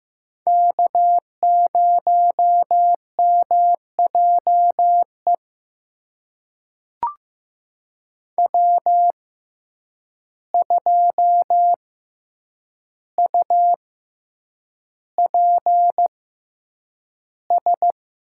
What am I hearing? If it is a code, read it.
Morse code, heard as a keyed tone, about 15 wpm, spaced as for 8 wpm, 700 Hz. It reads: K0MJE W2UPS
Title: USA Call Signs - 5 Characters - 6x Character Spacing - 15wpm